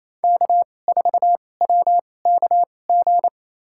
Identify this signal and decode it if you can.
Morse code, heard as a keyed tone, about 28 wpm, spaced as for 20 wpm, 700 Hz. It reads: K4WKG
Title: USA Call Signs - 5 Characters - 2x Character Spacing - 28wpm